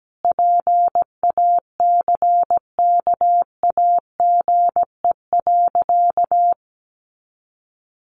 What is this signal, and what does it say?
Morse code, heard as a keyed tone, about 17 wpm, 700 Hz. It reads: PACKAGE.
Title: Additional Easy Phrases 17wpm